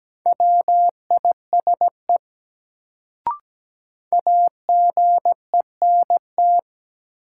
Morse code, heard as a keyed tone, about 17 wpm, 700 Hz. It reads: WISE AGENT
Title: Sentences from Top 1600 Words - New Word Drill Alt - 17wpm